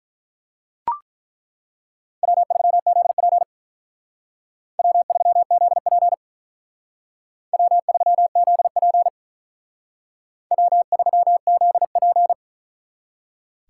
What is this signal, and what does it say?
Morse code, heard as a keyed tone, about 42 wpm, 700 Hz. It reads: W3ZP W3ZP W3ZP W3ZP
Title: Speed Racing - USA 1x2 Call Signs 35wpm